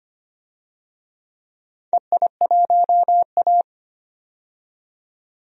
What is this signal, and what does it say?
Morse code, heard as a keyed tone, about 25 wpm, 700 Hz. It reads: EI1A
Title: DXCC-100 Call Signs - 4 Characters - 25wpm